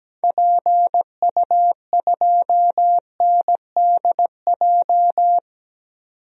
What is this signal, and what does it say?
Morse code, heard as a keyed tone, about 17 wpm, 700 Hz. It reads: PU2NDJ
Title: DXCC-100 Call Signs - 6 Characters - 17wpm